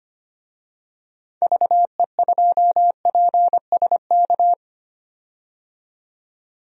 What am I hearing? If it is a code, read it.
Morse code, heard as a keyed tone, about 25 wpm, 700 Hz. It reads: VE2PSK
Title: DXCC-100 Call Signs - 6 Characters - 25wpm